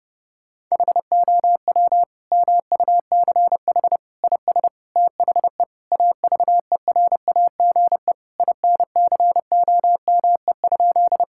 Morse code, heard as a keyed tone, about 30 wpm, 700 Hz. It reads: HOW MUCH IS THE AVERAGE INCOME?
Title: Sentences from Top 1000 Words - Review Part 6 - 30wpm